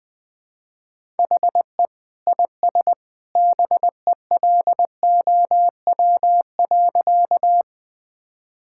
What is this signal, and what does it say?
Morse code, heard as a keyed tone, about 19 wpm, 700 Hz. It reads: HE IS BELOW.
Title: Sentences from Top 1000 Words - Review Part 5 - 20wpm